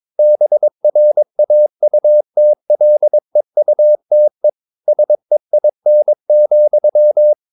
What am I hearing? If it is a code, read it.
Morse code, heard as a keyed tone, about 22 wpm, 600 Hz. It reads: BRAUTLEUTE SEIN,